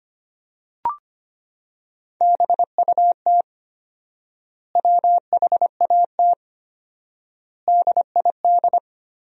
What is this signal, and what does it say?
Morse code, heard as a keyed tone, about 25 wpm, 700 Hz. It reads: BUT WHAT DID
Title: 4x Word-Spacing - Sentences from Top 200 Words - 25wpm